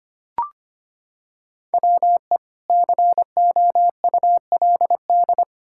Morse code, heard as a keyed tone, about 25 wpm, 700 Hz. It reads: WE COULD
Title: Pangram Challenge v2 - 25wpm